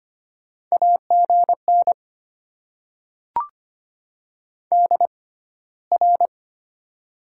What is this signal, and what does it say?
Morse code, heard as a keyed tone, about 25 wpm, spaced as for 15 wpm, 700 Hz. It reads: AGN DR
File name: Top 90 QSO Elements - 6x Character Spacing - 25wpm